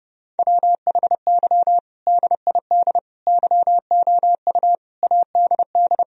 Morse code, heard as a keyed tone, about 32 wpm, 700 Hz. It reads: WHY DID YOU ADD
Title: Sentences from Top 1000 Words - Review Part 1 - 30wpm